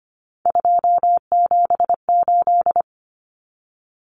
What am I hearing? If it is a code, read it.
Morse code, heard as a keyed tone, about 25 wpm, 700 Hz. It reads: 278
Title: Common Cut Numbers - 3-Digit 25wpm